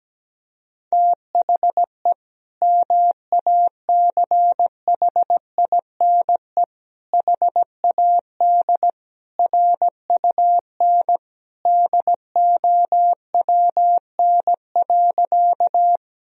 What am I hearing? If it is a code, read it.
Morse code, heard as a keyed tone, about 17 wpm, 700 Hz. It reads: THE MACHINE HAD RUN DOWN.
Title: Sentences from Top 1000 Words - Review Part 2 - 17wpm